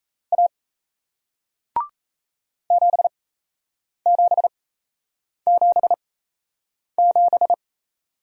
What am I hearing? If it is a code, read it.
Morse code, heard as a keyed tone, about 38 wpm, spaced as for 8 wpm, 700 Hz. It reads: A 7777